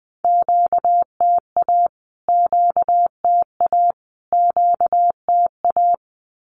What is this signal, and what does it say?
Morse code, heard as a keyed tone, about 20 wpm, 700 Hz. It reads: QTA QTA QTA
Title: Q-Codes Repeated 3x 20wpm